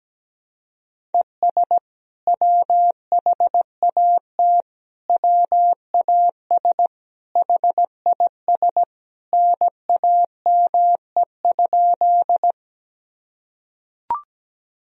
Morse code, heard as a keyed tone, about 17 wpm, 700 Hz. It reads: ES WHAT WAS HIS NAME?
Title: Sentences from Top 200 Words and Abbreviations 17wpm